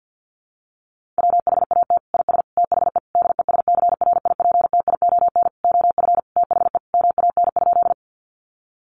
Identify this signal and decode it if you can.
Morse code, heard as a keyed tone, about 50 wpm, 700 Hz. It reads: WHAT IS THE DESCRIPTION OF THE MAN?
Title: Sentences from Top 1300 Words 50wpm